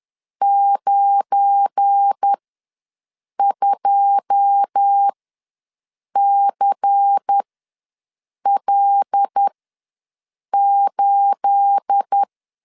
Morse code, heard as a keyed tone, about 10 wpm, spaced as for 6 wpm, 800 Hz. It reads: 92CL8
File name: LEZIONI / 9 2867 ws 10wpm 4215